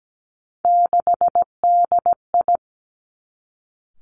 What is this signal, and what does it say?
Morse code, heard as a keyed tone, about 17 wpm, 700 Hz. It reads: 6DI
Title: Three Challenging Letter-Number Combinations 17wpm